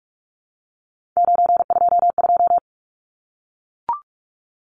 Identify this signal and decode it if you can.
Morse code, heard as a keyed tone, about 44 wpm, 700 Hz. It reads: 922
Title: Numbers - 3-Digit 45wpm